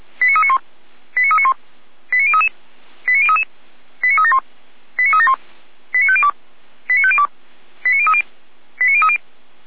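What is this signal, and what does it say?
Five-tone selective call (ZVEI), heard as quick runs of five tones, tones of about 70 ms. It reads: 89391 89391 89033 89033 89371 89371 89502 89502 89030 89030
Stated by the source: ZVEI five-tone selective call received on an SDR, from the feuerwehr-orlamuende.org fire brigade audio server.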